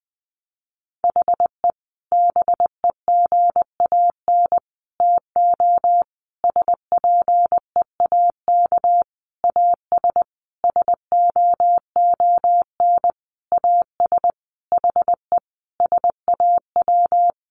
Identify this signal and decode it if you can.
Morse code, heard as a keyed tone, about 20 wpm, 700 Hz. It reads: HE BEGAN TO SPEAK AS SOON AS HE SAW
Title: Sentences from Top 1100 Words 20wpm